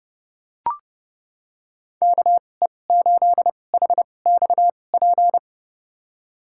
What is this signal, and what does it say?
Morse code, heard as a keyed tone, about 30 wpm, spaced as for 22 wpm, 700 Hz. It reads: KE8HXP
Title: USA Call Signs - 6 Characters - 2x Character Spacing - 30wpm